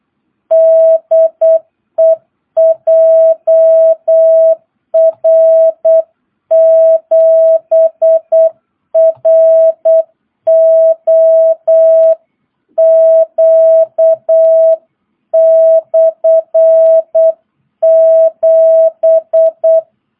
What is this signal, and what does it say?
Morse code, heard as a keyed tone, about 8 wpm, 660 Hz. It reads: DE JR7ROQ/7
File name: jh8cbh / 2022cw